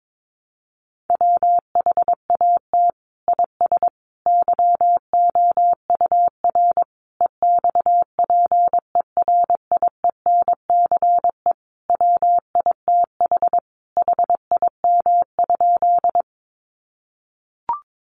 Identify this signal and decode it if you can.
Morse code, heard as a keyed tone, about 22 wpm, 700 Hz. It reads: WHAT IS YOUR EXPERIENCE WITH HIM?
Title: Sentences from Top 600 Words 22wpm